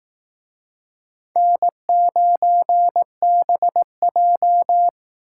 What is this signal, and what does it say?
Morse code, heard as a keyed tone, about 18 wpm, 700 Hz. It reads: N9BJ
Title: Speed Racing - USA 1x2 Call Signs 15wpm